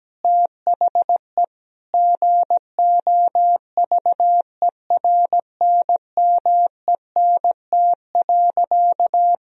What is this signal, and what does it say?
Morse code, heard as a keyed tone, about 17 wpm, 700 Hz. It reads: THE GOVERNMENT.
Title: Sentences from Top 1000 Words - Review Part 3 - 17wpm